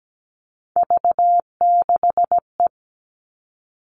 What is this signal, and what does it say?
Morse code, heard as a keyed tone, about 17 wpm, 700 Hz. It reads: V6E
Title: All Cut Numbers - 3-Digit 17wpm